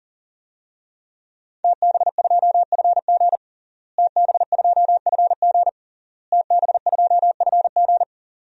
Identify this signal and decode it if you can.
Morse code, heard as a keyed tone, about 40 wpm, 700 Hz. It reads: TB2FG TB2FG TB2FG
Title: DXCC-100 Call Signs - 5 Characters - Repeated 3x - 40wpm